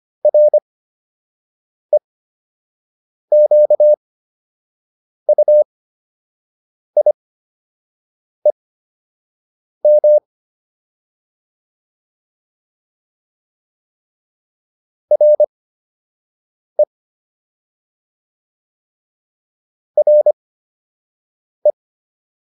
Morse code, heard as a keyed tone, about 25 wpm, spaced as for 6 wpm, 600 Hz. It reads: REQUIEM RE RE
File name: RE7 06